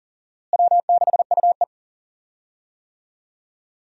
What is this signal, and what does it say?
Morse code, heard as a keyed tone, about 37 wpm, 700 Hz. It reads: W6UE
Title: USA Call Signs - 4 Characters - Repeated 3x - 40wpm